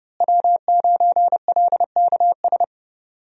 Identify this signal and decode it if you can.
Morse code, heard as a keyed tone, about 30 wpm, 700 Hz. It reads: W9LKS
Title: USA Call Signs - 5 Characters - Repeated 2x - 30wpm